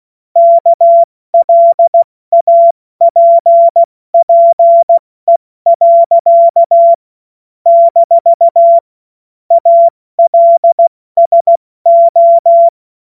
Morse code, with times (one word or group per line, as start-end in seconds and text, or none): 0.4-6.9 KLAPPE.
7.7-8.8 -
9.5-12.7 ALSO